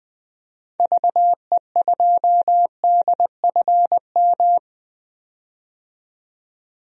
Morse code, heard as a keyed tone, about 20 wpm, 700 Hz. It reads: VE2DFM
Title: DXCC-100 Call Signs - 6 Characters - 20wpm